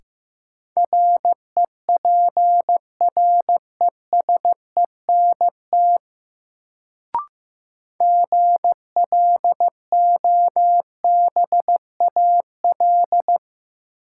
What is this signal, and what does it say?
Morse code, heard as a keyed tone, about 15 wpm, 700 Hz. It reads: REPRESENT GLOBAL
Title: Sentences from Top 1200 Words - New Word Drill - 15wpm